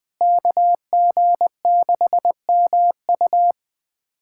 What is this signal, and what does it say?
Morse code, heard as a keyed tone, about 21 wpm, 700 Hz. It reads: KG6MU
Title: USA Callsign DE Callsign - 5 Characters - 20wpm